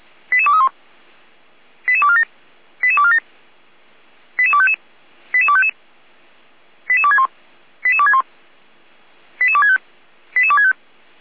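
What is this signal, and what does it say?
Five-tone selective call (ZVEI), heard as quick runs of five tones, tones of about 70 ms. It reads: 88231 88247 88247 88255 88255 88261 88261 88265 88265
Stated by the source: ZVEI five-tone call received by SDR.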